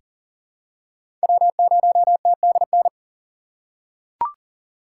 Morse code, heard as a keyed tone, about 40 wpm, 700 Hz. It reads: W0TDN
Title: USA Call Signs - 5 Characters - 6x Character Spacing - 40wpm